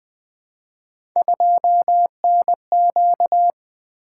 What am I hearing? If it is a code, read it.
Morse code, heard as a keyed tone, about 21 wpm, 700 Hz. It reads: 2NQ
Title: Three Letter-Number Combinations - 20wpm